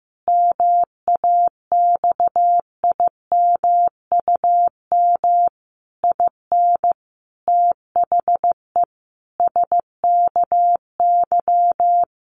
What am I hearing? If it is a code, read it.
Morse code, heard as a keyed tone, about 15 wpm, 700 Hz. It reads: MAXIMUM IN THE SKY
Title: Sentences from Top 1300 Words 15wpm